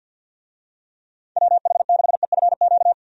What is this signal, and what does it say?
Morse code, heard as a keyed tone, about 46 wpm, 700 Hz. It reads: WH6EFQ
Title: USA Call Signs - 6 Characters - 50wpm